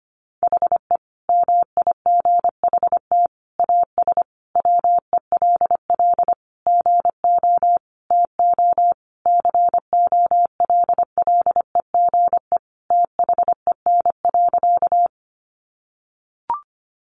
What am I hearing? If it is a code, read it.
Morse code, heard as a keyed tone, about 25 wpm, 700 Hz. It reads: HE MIGHT AS WELL GO TO COLLEGE THEN.
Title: Sentences from Top 1300 Words 25wpm